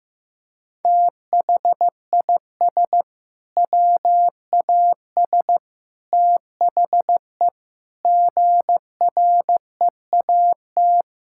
Morse code, heard as a keyed tone, about 16 wpm, 700 Hz. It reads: THIS WAS THE GREAT
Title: Sentences from Top 1000 Words - Review Part 5 - 15wpm